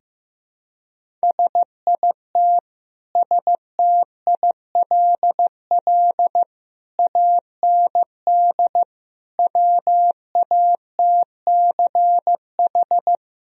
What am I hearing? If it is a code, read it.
Morse code, heard as a keyed tone, about 15 wpm, 700 Hz. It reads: SIT STILL AND WATCH
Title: Sentences from Top 1000 Words - Review Part 1 - 15wpm